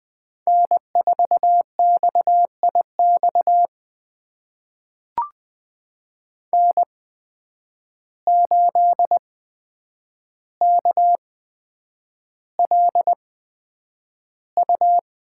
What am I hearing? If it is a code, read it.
Morse code, heard as a keyed tone, about 20 wpm, spaced as for 9 wpm, 700 Hz. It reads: N4XIX N8KLU
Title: USA Call Signs - 5 Characters - 8x Character Spacing - 20wpm